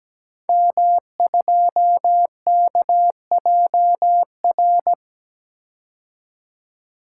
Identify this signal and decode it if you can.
Morse code, heard as a keyed tone, about 17 wpm, 700 Hz. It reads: M2KJR
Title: Code Groups - 5 Letter:Numbers - 17wpm